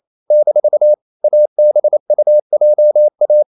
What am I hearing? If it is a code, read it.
Morse code, heard as a keyed tone, about 28 wpm, 600 Hz. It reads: - ABUJA
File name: CW / Practice / NationalCapitals28